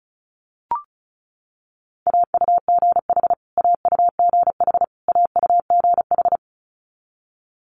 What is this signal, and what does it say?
Morse code, heard as a keyed tone, about 36 wpm, 700 Hz. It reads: AUGH AUGH AUGH
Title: Multi-letter Phonemes Repeated 3x 35wpm